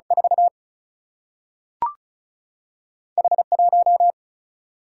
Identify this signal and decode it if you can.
Morse code, heard as a keyed tone, about 36 wpm, 700 Hz. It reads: 4 H 1
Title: Single-Letter Single-Number 35wpm